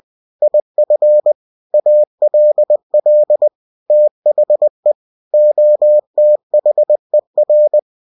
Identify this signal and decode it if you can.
Morse code, heard as a keyed tone, about 20 wpm, 600 Hz. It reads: IF ALL THE OTHER
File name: cw-books / winnie-the-pooh / winnie-20wpm-page-0103